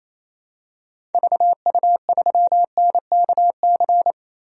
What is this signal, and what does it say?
Morse code, heard as a keyed tone, about 28 wpm, 700 Hz. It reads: VU3NKC
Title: DXCC-100 Call Signs - 6 Characters - Repeated 3x - 28wpm